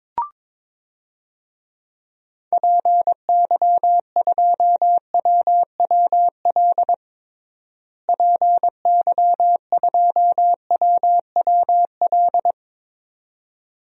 Morse code, heard as a keyed tone, about 22 wpm, 700 Hz. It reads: PY2WWL PY2WWL
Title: DXCC-100 Call Signs - 6 Characters - Repeated 2x - 22wpm